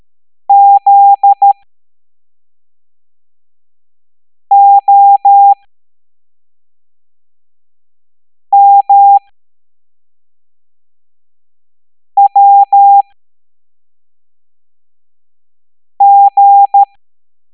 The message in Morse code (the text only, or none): ZOMWG